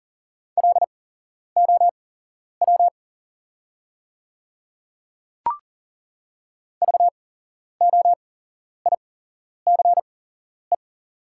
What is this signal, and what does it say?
Morse code, heard as a keyed tone, about 40 wpm, spaced as for 11 wpm, 700 Hz. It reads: LOW VOICE
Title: 8x Character Spacing - Top 500 Words - 40wpm